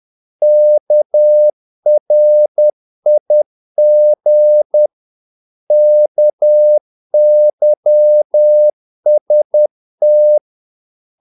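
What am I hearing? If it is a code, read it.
Morse code, heard as a keyed tone, about 10 wpm, 600 Hz. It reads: KRIG KYST
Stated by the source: Morse practice of short Danish words at 10 wpm, from CW med Gnister.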